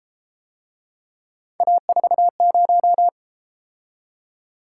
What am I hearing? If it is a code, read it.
Morse code, heard as a keyed tone, about 31 wpm, 700 Hz. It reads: A40